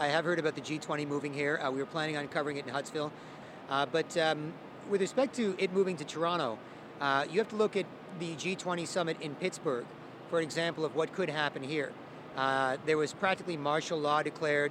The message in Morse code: none

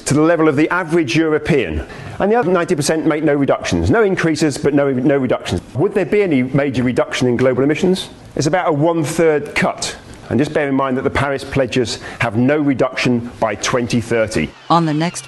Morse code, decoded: none

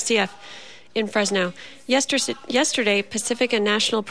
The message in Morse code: none

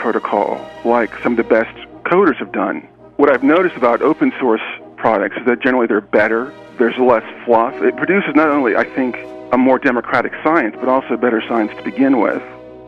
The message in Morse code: none

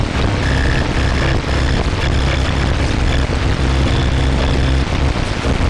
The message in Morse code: none

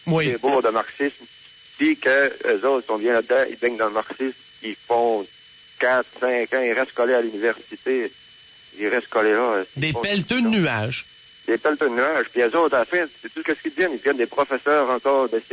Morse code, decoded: none

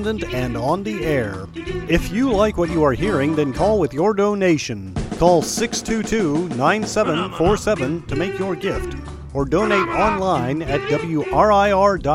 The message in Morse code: none

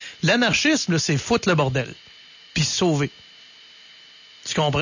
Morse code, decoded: none

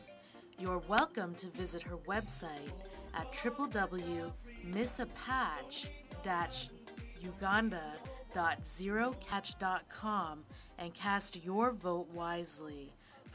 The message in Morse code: none

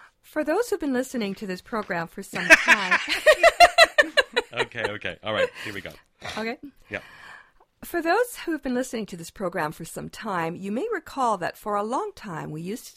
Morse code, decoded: none

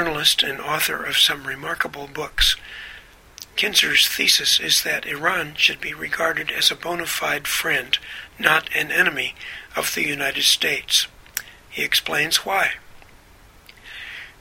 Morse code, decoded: none

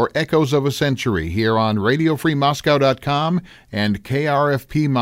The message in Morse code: none

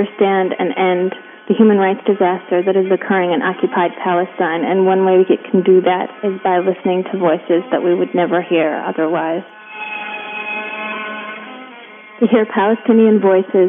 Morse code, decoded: none

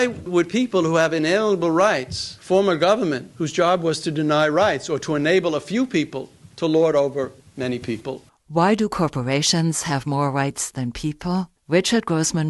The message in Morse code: none

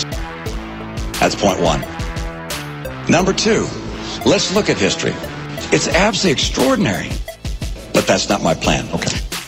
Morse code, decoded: none

